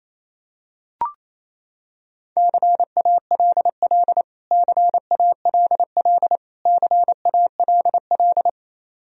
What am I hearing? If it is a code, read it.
Morse code, heard as a keyed tone, about 28 wpm, 700 Hz. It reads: CALL CALL CALL
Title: Top 100 Words Repeated 28wpm